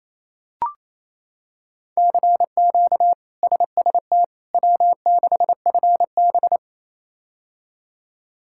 Morse code, heard as a keyed tone, about 28 wpm, 700 Hz. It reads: CQ SST W6FB
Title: K1USN SST QSO Practice - 28wpm